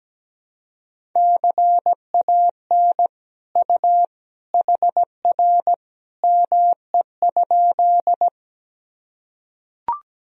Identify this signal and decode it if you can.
Morse code, heard as a keyed tone, about 17 wpm, 700 Hz. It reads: CAN U HR ME?
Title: Sentences from Top 300 Words and Abbreviations 17wpm